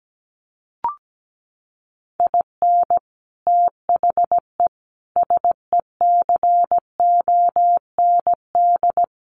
Morse code, heard as a keyed tone, about 17 wpm, 700 Hz. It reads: IN THE SECOND
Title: Sets of 5 Words 17wpm